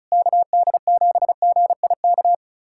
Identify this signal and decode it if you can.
Morse code, heard as a keyed tone, about 36 wpm, 700 Hz. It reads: KD7GIK
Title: USA Callsign DE Callsign - 6 Characters - 35wpm